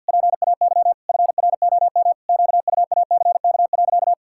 Morse code, heard as a keyed tone, about 53 wpm, 700 Hz. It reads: PAY FROM QUACK.
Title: Pangram Challenge v2 - 50wpm